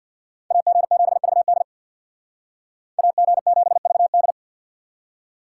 Morse code, heard as a keyed tone, about 65 wpm, 700 Hz. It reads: AG7VD AG7VD